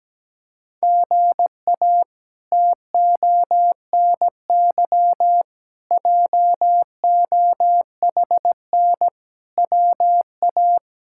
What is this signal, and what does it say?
Morse code, heard as a keyed tone, about 17 wpm, 700 Hz. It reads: GA TONY JOHN WA
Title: K1USN SST QSO Practice - 17wpm